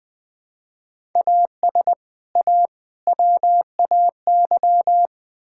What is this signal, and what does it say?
Morse code, heard as a keyed tone, about 21 wpm, 690 Hz. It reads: AS A WAY
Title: Build A Phrase - 4-Word Phrases - 20wpm